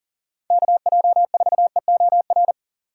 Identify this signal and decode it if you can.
Morse code, heard as a keyed tone, about 40 wpm, 700 Hz. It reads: KJ4EOR
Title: USA Call Signs - 6 Characters - 4x Character Spacing - 40wpm